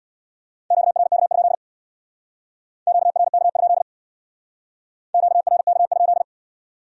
Morse code, heard as a keyed tone, about 62 wpm, 700 Hz. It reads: QRK? QRK? QRK?
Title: Speed-Racing Q-Codes - 50wpm